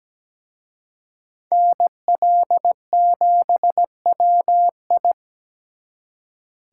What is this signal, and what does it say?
Morse code, heard as a keyed tone, about 20 wpm, 700 Hz. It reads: NL7WI
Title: Speed Racing - USA 2x2 Call Signs 17wpm